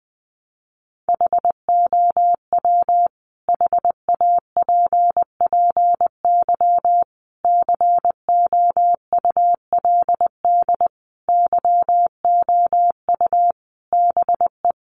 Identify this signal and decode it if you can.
Morse code, heard as a keyed tone, about 20 wpm, 700 Hz. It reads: HOW HAPPY COULD YOU BE
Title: Sentences from Top 1000 Words 20wpm